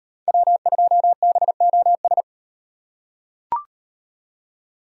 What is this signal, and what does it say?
Morse code, heard as a keyed tone, about 36 wpm, 700 Hz. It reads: W2BOS